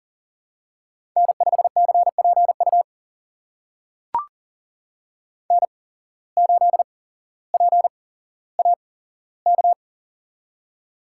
Morse code, heard as a keyed tone, about 41 wpm, spaced as for 17 wpm, 700 Hz. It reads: N5CPU N8PAK
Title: USA Call Signs - 5 Characters - 8x Character Spacing - 40wpm